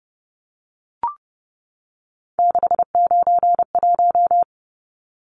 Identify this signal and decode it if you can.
Morse code, heard as a keyed tone, about 30 wpm, 700 Hz. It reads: 691
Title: Numbers - 3-Digit 30wpm